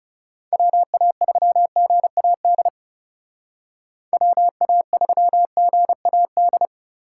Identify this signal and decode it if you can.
Morse code, heard as a keyed tone, about 39 wpm, 700 Hz. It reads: WA3GAD WA3GAD